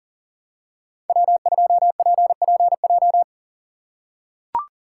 Morse code, heard as a keyed tone, about 39 wpm, 700 Hz. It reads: W2PPJ
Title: USA Call Signs - 5 Characters - 4x Character Spacing - 40wpm